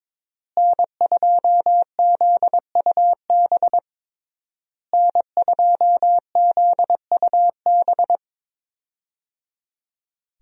Morse code, heard as a keyed tone, about 21 wpm, 700 Hz. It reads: N2ZUB N2ZUB
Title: USA Call Signs - 5 Characters - Repeated 3x - 22wpm